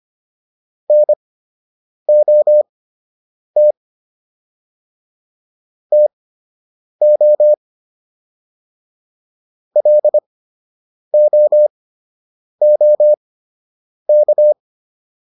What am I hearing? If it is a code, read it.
Morse code, heard as a keyed tone, about 25 wpm, spaced as for 8 wpm, 600 Hz. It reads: NOT TO LOOK